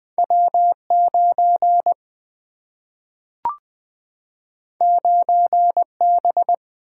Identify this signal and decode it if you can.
Morse code, heard as a keyed tone, about 20 wpm, 700 Hz. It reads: W 9 9 B
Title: Single-Letter Single-Number 20wpm